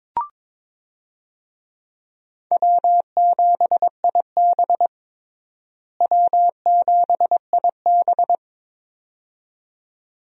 Morse code, heard as a keyed tone, about 22 wpm, 700 Hz. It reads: W7IB W7IB
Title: USA Call Signs - 4 Characters - Repeated 2x - 22wpm